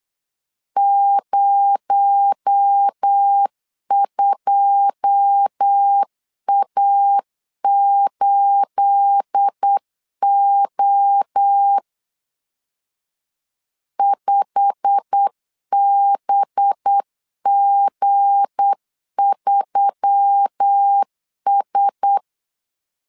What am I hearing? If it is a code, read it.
Morse code, heard as a keyed tone, about 8 wpm, 800 Hz. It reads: 02A8O 5BG3S
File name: LEZIONI / 9 28 8wpm 3512